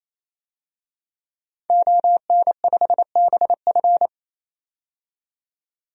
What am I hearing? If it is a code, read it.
Morse code, heard as a keyed tone, about 28 wpm, 700 Hz. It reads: ON5BF
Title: DXCC-100 Call Signs - 5 Characters - 28wpm